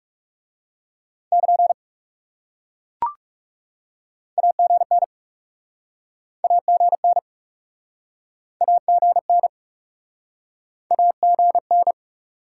Morse code, heard as a keyed tone, about 40 wpm, 700 Hz. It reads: <KN> AGN AGN AGN AGN